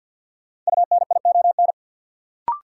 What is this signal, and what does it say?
Morse code, heard as a keyed tone, about 47 wpm, 700 Hz. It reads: UNION
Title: Sets of 5 Words Encore - New Word Drill - 50wpm